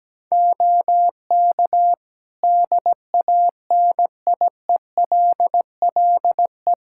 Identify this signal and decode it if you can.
Morse code, heard as a keyed tone, about 17 wpm, 700 Hz. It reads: OK DANIELLE
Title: Practice QSO - Exchange 3 - More Info - 17wpm